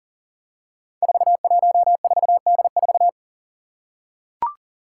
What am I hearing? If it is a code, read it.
Morse code, heard as a keyed tone, about 40 wpm, 700 Hz. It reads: 414D4
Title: Five Challenging Letter-Number Combinations 40wpm